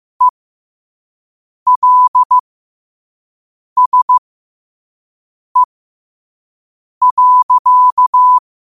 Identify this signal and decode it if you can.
Morse code, heard as a keyed tone, about 15 wpm, spaced as for 5 wpm, 1000 Hz. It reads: ELSE.
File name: qotdfiles / 5wpm